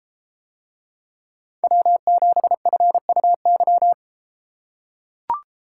Morse code, heard as a keyed tone, about 33 wpm, 700 Hz. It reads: W7FUY